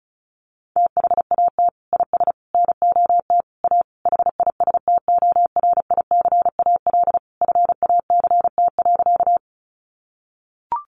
Morse code, heard as a keyed tone, about 34 wpm, 700 Hz. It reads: THAT IS NOT A HISTORICAL FACT.
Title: Sentences from Top 600 Words 35wpm